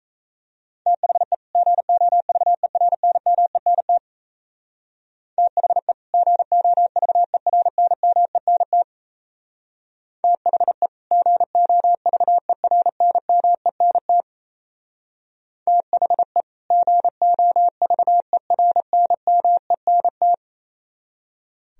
Morse code, heard as a keyed tone, about 35 wpm, 700 Hz. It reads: THE GOVERNMENT THE GOVERNMENT THE GOVERNMENT THE GOVERNMENT
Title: Speed-Racing - Sets of 2 Words 28wpm